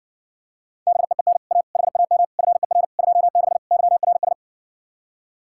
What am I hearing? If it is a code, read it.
Morse code, heard as a keyed tone, about 68 wpm, 700 Hz. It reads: BEEN A HAM FER 26 YRS